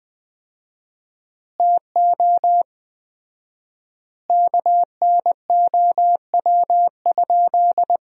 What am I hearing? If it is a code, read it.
Morse code, heard as a keyed tone, about 20 wpm, 700 Hz. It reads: TO KNOW?
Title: 4x Word-Spacing - Sentences from Top 200 Words - 20wpm